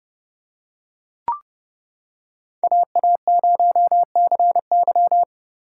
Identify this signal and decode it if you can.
Morse code, heard as a keyed tone, about 29 wpm, 700 Hz. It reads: AA0CY